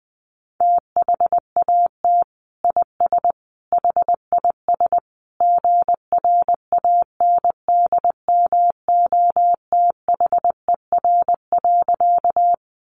Morse code, heard as a keyed tone, about 20 wpm, 700 Hz. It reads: THAT IS HIS GRANDMOTHER.
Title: Sentences from Top 1200 Words 20wpm